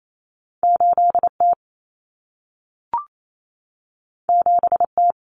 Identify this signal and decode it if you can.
Morse code, heard as a keyed tone, about 28 wpm, 700 Hz. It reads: 8T 7T
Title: Common Cut Numbers - 2-Digit 28wpm